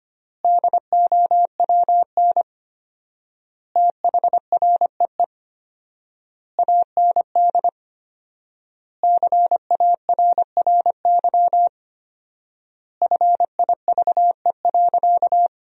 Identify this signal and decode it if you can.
Morse code, heard as a keyed tone, about 25 wpm, 700 Hz. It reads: DOWN THREE AND CARRY FIVE.
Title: 4x Word-Spacing - Sentences from Top 400 Words - 25wpm